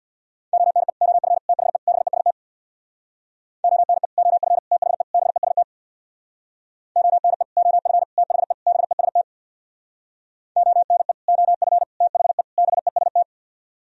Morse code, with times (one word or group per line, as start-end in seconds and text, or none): 0.5-0.9 ONE
1.0-1.4 OF
1.5-1.8 THE
1.9-2.3 BEST
3.6-4.1 ONE
4.2-4.6 OF
4.7-5.0 THE
5.1-5.6 BEST
7.0-7.4 ONE
7.6-8.0 OF
8.2-8.5 THE
8.7-9.2 BEST
10.6-11.1 ONE
11.3-11.8 OF
12.0-12.4 THE
12.6-13.2 BEST